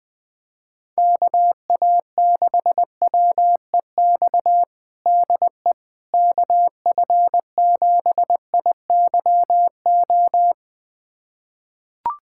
Morse code, heard as a keyed tone, about 20 wpm, 700 Hz. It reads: KA6WEX DE KF7IYO
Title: USA Callsign DE Callsign - 6 Characters - 20wpm